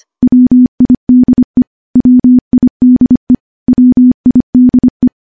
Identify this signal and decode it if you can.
Morse code, heard as a keyed tone, about 25 wpm, 250 Hz. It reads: WIDE WIDE WIDE
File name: morse / morse-training-25WPM